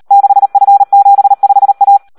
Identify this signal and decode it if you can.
Morse code, heard as a keyed tone, about 38 wpm, 800 Hz. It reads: 6F75A
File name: Audio practica CW / SD 1 38